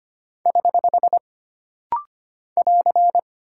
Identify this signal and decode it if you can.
Morse code, heard as a keyed tone, about 25 wpm, 700 Hz. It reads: <HH> <AR>
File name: Single Punctuation-Prosign - Rapid-Fire - 25wpm